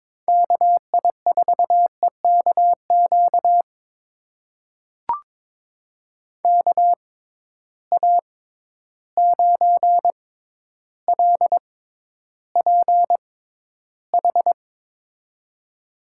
Morse code, heard as a keyed tone, about 22 wpm, spaced as for 11 wpm, 700 Hz. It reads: KI4EKQ KA9LPH
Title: USA Call Signs - 6 Characters - 6x Character Spacing - 22wpm